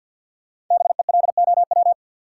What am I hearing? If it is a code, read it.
Morse code, heard as a keyed tone, about 47 wpm, 700 Hz. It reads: BELOW
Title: Sentences from Top 1200 Words - New Word Drill Alt - 50wpm